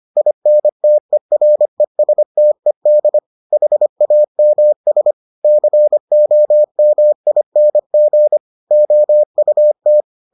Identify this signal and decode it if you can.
Morse code, heard as a keyed tone, about 25 wpm, 600 Hz. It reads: INTERESTED HAMS COMING OUT